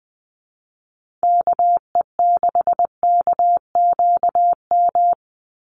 Morse code, heard as a keyed tone, about 20 wpm, 700 Hz. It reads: KE6KQM
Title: USA Call Signs 20wpm